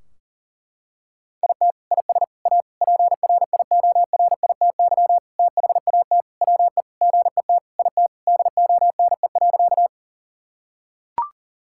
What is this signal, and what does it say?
Morse code, heard as a keyed tone, about 40 wpm, 700 Hz. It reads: IT IS A PRIORITY THAT WE GET IT DONE.
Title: Sentences from Top 1600 Words 40wpm